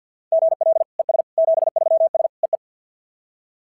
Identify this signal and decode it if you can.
Morse code, heard as a keyed tone, about 47 wpm, 690 Hz. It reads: GL ES 73S EE
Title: SOTA QSO Practice 50wpm